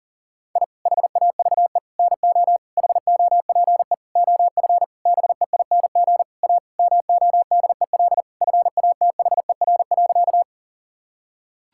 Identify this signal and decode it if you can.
Morse code, heard as a keyed tone, about 40 wpm, 700 Hz. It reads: I HAVE NO HOPE OF BEING A MODEL FATHER.
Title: Sentences from Top 1000 Words - Review Part 5 - 40wpm